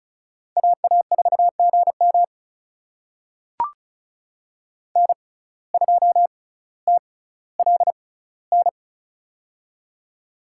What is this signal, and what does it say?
Morse code, heard as a keyed tone, about 36 wpm, spaced as for 18 wpm, 700 Hz. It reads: AA4GM N2TLN
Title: USA Call Signs - 5 Characters - 6x Character Spacing - 35wpm